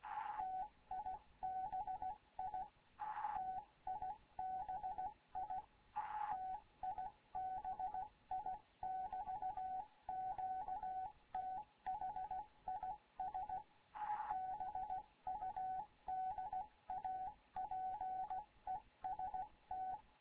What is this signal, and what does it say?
Morse code, heard as a keyed tone, about 16 wpm, 750 Hz. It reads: TIBI TIBI TIBI <BT> QTH IS BUDAPEST
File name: Morze-HA2MN / hangp / 2 11 GYAKORLAT